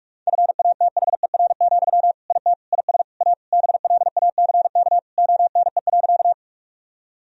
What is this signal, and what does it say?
Morse code, heard as a keyed tone, about 46 wpm, 700 Hz. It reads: FATHER, IT IS A BLACK ONE.
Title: Sentences from Top 1000 Words - Review Part 2 - 45wpm